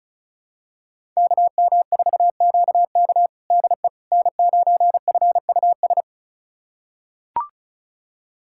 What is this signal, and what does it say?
Morse code, heard as a keyed tone, about 34 wpm, 700 Hz. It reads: KM4QK DE N9FUS
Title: USA Callsign DE Callsign - 5 Characters - 35wpm